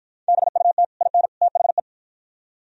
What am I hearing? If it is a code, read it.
Morse code, heard as a keyed tone, about 48 wpm, 700 Hz. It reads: BUT IN THE